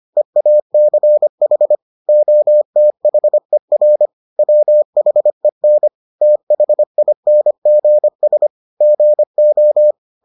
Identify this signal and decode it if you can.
Morse code, heard as a keyed tone, about 25 wpm, 600 Hz. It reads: EACH OTHER WHEN THINGS GO